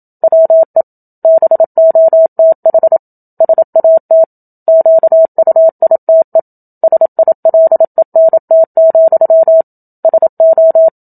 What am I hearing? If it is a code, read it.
Morse code, heard as a keyed tone, about 27 wpm, 650 Hz. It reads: WE BOTH SAT QUITE SILENT, SO
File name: Book Two / 01-Chapter 1 - 27 WPM